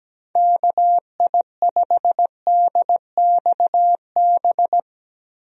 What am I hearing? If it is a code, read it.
Morse code, heard as a keyed tone, about 17 wpm, 700 Hz. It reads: KI5DXB
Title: USA Call Signs - 6 Characters - 17wpm